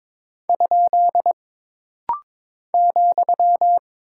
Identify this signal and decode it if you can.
Morse code, heard as a keyed tone, about 22 wpm, 700 Hz. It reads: ? ,
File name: practice / 22wpm / Single Punctuation-Prosign - Rapid-Fire - 22wpm